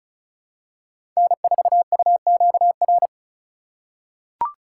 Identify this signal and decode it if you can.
Morse code, heard as a keyed tone, about 36 wpm, 700 Hz. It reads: N4UQR
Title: USA Call Signs - 5 Characters - 3x Character Spacing - 35wpm